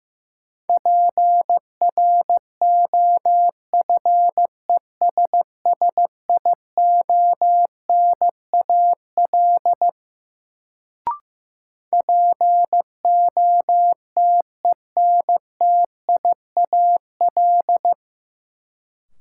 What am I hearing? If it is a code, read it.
Morse code, heard as a keyed tone, about 15 wpm, 700 Hz. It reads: PROFESSIONAL POTENTIAL
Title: Sentences from Top 1000 Words - Review Part 4 - New Word Drill - 15wpm